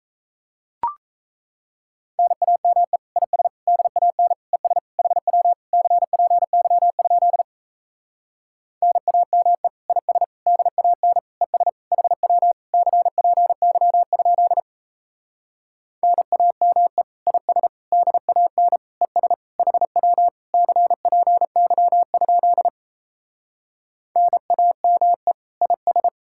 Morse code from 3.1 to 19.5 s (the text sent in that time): IS DAN ES HW CPY? NAME IS DAN ES HW CPY? NAME IS DAN ES